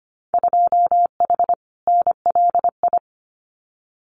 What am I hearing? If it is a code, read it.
Morse code, heard as a keyed tone, about 25 wpm, 700 Hz. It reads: 2H NLI
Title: Winter Field Day Exchanges Encore 25wpm